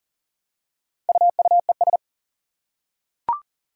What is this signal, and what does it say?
Morse code, heard as a keyed tone, about 38 wpm, 700 Hz. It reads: UUES
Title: Four Challenging Letter-Number Combinations 40wpm